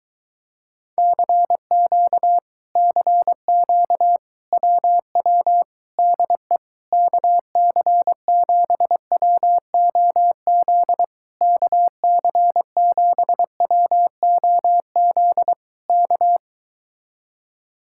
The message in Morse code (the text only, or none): CQ CQ WW DE KC7WOZ KC7WOZ K